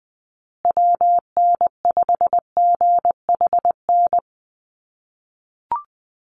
Morse code, heard as a keyed tone, about 20 wpm, 700 Hz. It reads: WN5GHN
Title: USA Call Signs Repeated 2x 20wpm